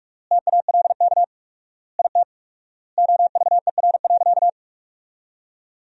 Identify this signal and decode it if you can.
Morse code, heard as a keyed tone, about 46 wpm, 700 Hz. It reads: TALK IT OVER.
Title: 4x Word-Spacing - Sentences from Top 500 Words - 45wpm